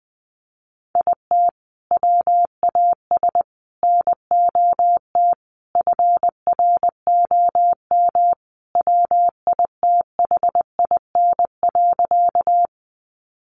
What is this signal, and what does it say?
Morse code, heard as a keyed tone, about 20 wpm, 700 Hz. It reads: IT WAS NOT FROM WITHIN.
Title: Sentences from Top 700 Words 20wpm